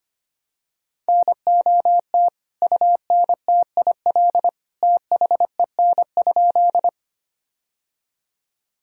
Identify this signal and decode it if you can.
Morse code, heard as a keyed tone, about 25 wpm, 700 Hz. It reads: NOT UNTIL THEN?
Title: Sentences from Top 1000 Words - Review Part 1 - 25wpm